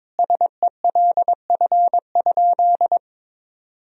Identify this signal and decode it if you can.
Morse code, heard as a keyed tone, about 23 wpm, 700 Hz. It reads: SELF?
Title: Sentences from Top 1000 Words - Review Part 1 - 22wpm